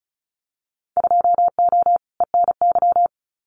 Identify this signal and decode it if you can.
Morse code, heard as a keyed tone, about 36 wpm, 700 Hz. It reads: 2O ENY
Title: Winter Field Day Exchanges 35wpm